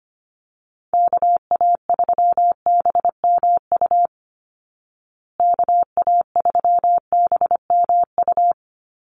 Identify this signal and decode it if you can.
Morse code, heard as a keyed tone, about 25 wpm, 700 Hz. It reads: KA3BMU KA3BMU
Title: Call Signs Repeated 2x - 25wpm